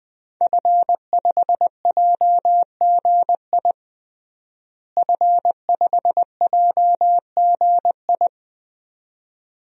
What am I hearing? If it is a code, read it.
Morse code, heard as a keyed tone, about 21 wpm, 700 Hz. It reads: F5JGI F5JGI
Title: DXCC-100 Call Signs - 5 Characters - Repeated 2x - 20wpm